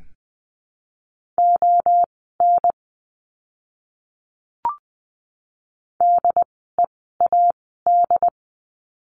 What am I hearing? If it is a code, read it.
Morse code, heard as a keyed tone, about 20 wpm, spaced as for 15 wpm, 700 Hz. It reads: ON DEAD
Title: 075 Lesson 13f - Letter L - Words 20wpm